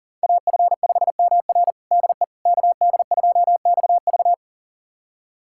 Morse code, heard as a keyed tone, about 40 wpm, 700 Hz. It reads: AF5MR DE KD2XV
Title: USA Callsign DE Callsign - 5 Characters - 40wpm